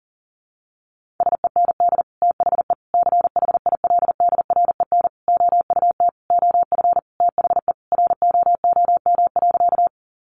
Morse code, heard as a keyed tone, about 40 wpm, 700 Hz. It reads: SEND THE CHILDREN OUT OF THE ROOM.
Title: Sentences from Top 1500 Words 40wpm